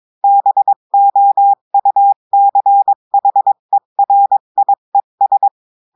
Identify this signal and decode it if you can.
Morse code, heard as a keyed tone, about 22 wpm, 800 Hz. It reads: BOUCHERIES
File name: cours-du-mercredi / audios / cours-853-du-2-mai-2024-etxte chiffre et texte clair-a-20-wpm